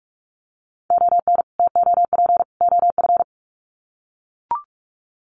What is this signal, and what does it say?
Morse code, heard as a keyed tone, about 44 wpm, 700 Hz. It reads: ON TOP OF
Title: Sets of 3 Words 45wpm